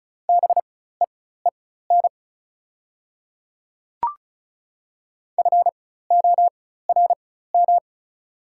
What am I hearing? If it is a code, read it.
Morse code, heard as a keyed tone, about 34 wpm, spaced as for 16 wpm, 690 Hz. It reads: BEEN FORM
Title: 4x Character Spacing - Top 300 Words - 35wpm